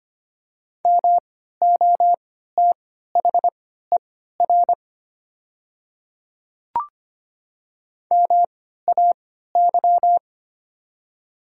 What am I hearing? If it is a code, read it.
Morse code, heard as a keyed tone, about 25 wpm, spaced as for 14 wpm, 700 Hz. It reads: MOTHER MAY
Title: 3x Character Spacing - Top 200 Words - 25wpm